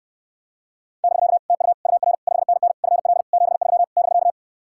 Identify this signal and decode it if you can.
Morse code, heard as a keyed tone, about 69 wpm, 700 Hz. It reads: <BK> TU UR 5NN FL 73 <BK>